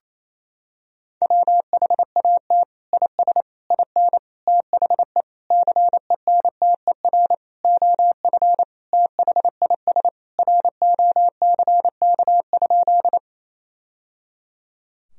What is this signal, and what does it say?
Morse code, heard as a keyed tone, about 27 wpm, 700 Hz. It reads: WHAT IS IN THE CENTER OF THIS ROCK?
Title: Sentences from Top 1000 Words - Review Part 2 - 28wpm